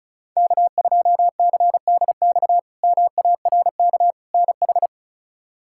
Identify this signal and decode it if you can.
Morse code, heard as a keyed tone, about 36 wpm, 700 Hz. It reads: K2CDX MARK NH
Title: K1USN SST QSO Practice - 35wpm